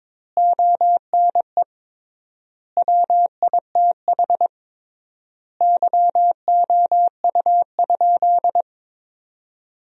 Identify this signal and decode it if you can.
Morse code, heard as a keyed tone, about 22 wpm, 700 Hz. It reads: ONE WITH YOU?
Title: 3x Word-Spacing - Sentences from Top 500 Words - 22wpm